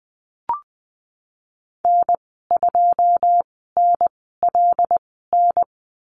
Morse code, heard as a keyed tone, about 20 wpm, spaced as for 15 wpm, 700 Hz. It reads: N2NLN
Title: 088 Lesson 15g - Number 5 - Call Signs 20wpm